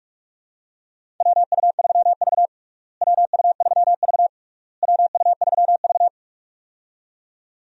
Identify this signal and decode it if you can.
Morse code, heard as a keyed tone, about 41 wpm, 700 Hz. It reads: WU3V WU3V WU3V
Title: USA Call Signs - 4 Characters - Repeated 3x - 45wpm